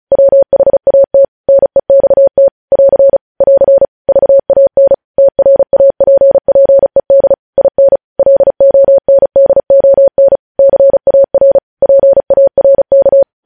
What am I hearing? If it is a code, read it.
Morse code, heard as a keyed tone, about 36 wpm, 550 Hz. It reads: WHAT NEXT <AR> <AR> VAN TRAPPED IN LONDON CAR PARK
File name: podcast / News headlines-1x-35x35